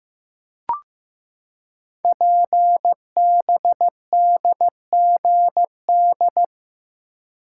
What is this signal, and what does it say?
Morse code, heard as a keyed tone, about 15 wpm, 700 Hz. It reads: PBDGD
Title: Code Groups - 5 Letters - 15wpm